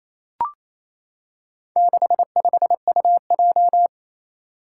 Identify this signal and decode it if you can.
Morse code, heard as a keyed tone, about 29 wpm, 700 Hz. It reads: 65UJ
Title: Four Challenging Letter-Number Combinations 28wpm